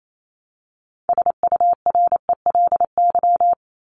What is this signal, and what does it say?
Morse code, heard as a keyed tone, about 28 wpm, 700 Hz. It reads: SURELY
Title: LY Suffix - Top 100 Words with Base Words 28wpm